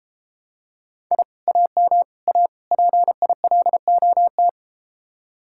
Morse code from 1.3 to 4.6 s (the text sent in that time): AM A PILOT